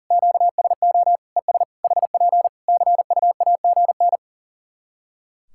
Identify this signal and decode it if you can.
Morse code, heard as a keyed tone, about 40 wpm, 700 Hz. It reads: QSO ES HP CUAGN
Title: Practice QSO - Exchange 4 - Ending - 40wpm